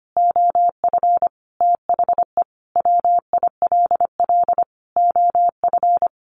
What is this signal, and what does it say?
Morse code, heard as a keyed tone, about 25 wpm, 700 Hz. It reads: OF THE WILL OF
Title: Sentences from Top 1500 Words 25wpm